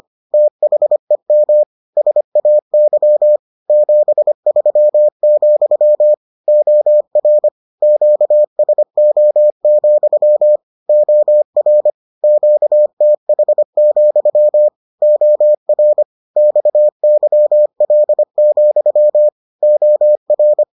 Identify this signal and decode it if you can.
Morse code, heard as a keyed tone, about 25 wpm, 600 Hz. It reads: THEM SAY 73, OR QSO, OR QTH, OR XYL, OR